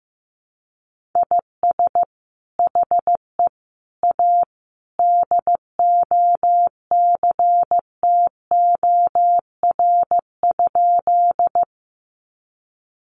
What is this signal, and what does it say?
Morse code, heard as a keyed tone, about 15 wpm, 700 Hz. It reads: IS HE A DOCTOR?
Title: Sentences from Top 1500 Words 15wpm